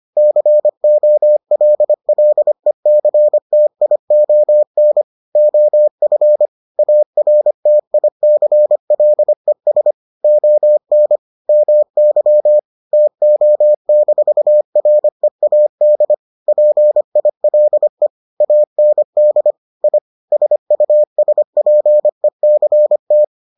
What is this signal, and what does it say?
Morse code, heard as a keyed tone, about 25 wpm, 600 Hz. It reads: COLLECTION OF ARTICLES ON MY TO-READ PILE AND I SUSPECT